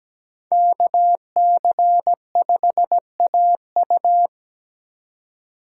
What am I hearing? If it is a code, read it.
Morse code, heard as a keyed tone, about 17 wpm, 700 Hz. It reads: KC5AU
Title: USA Call Signs - 3x Character Spacing - 17wpm